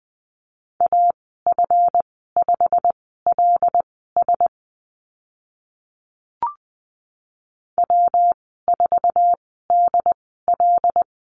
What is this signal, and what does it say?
Morse code, heard as a keyed tone, about 20 wpm, spaced as for 14 wpm, 700 Hz. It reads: AF5LS W4DL
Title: 140 Lesson 23g - Letter f - Call Signs 20wpm